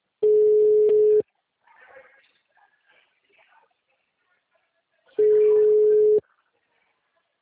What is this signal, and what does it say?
Ringback tone (Europe, 425 Hz).